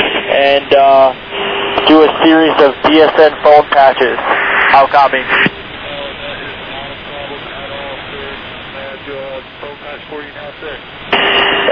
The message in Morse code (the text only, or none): none